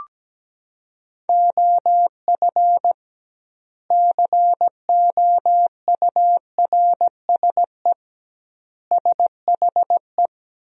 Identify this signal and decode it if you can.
Morse code, heard as a keyed tone, about 17 wpm, 700 Hz. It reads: OF COURSE SHE
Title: 2x Word-Spacing - Sentences from Top 500 Words - 17wpm